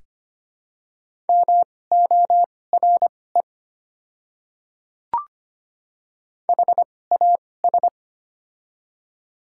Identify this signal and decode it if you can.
Morse code, heard as a keyed tone, about 25 wpm, spaced as for 18 wpm, 690 Hz. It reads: MORE HAS
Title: 2x Character Spacing - Top 100 Words - 25wpm